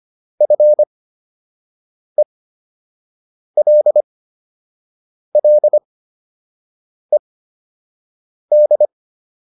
Morse code, heard as a keyed tone, about 25 wpm, spaced as for 6 wpm, 600 Hz. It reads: FELLED